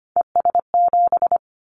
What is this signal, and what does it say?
Morse code, heard as a keyed tone, about 25 wpm, 700 Hz. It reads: ES7
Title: Callsign Prefixes for Top 100 DXCC - 25wpm